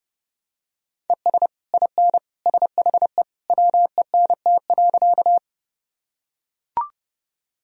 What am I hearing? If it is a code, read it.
Morse code, heard as a keyed tone, about 32 wpm, 700 Hz. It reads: ES IN SHE WENT.
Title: Sentences from Top 200 Words and Abbreviations 30wpm